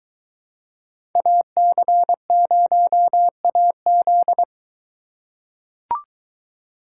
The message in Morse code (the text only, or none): AC0AZ